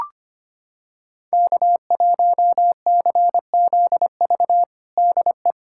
Morse code, heard as a keyed tone, about 25 wpm, 700 Hz. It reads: K1CZV DE
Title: USA Callsign DE Callsign - 5 Characters - 25wpm